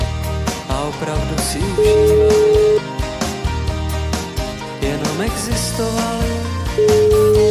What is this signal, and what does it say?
Ringback tone (Europe, 425 Hz), with noise.